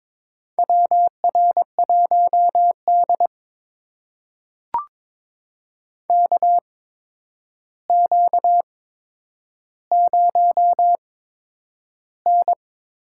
Morse code, heard as a keyed tone, about 23 wpm, spaced as for 10 wpm, 700 Hz. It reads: WR1D KQ0N
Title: USA Call Signs - 4 Characters - 8x Character Spacing - 22wpm